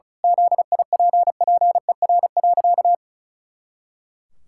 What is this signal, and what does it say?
Morse code, heard as a keyed tone, about 36 wpm, 700 Hz. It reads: ZIPPER.
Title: Pangram Challenge v2 - 35wpm